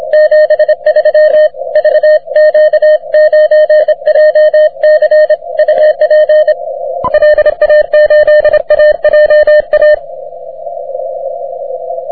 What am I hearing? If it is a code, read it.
Morse code, heard as a keyed tone, about 26 wpm, 610 Hz. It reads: 73 VQ9JC UP LA8AJA
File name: wp / vq9jc 30m cw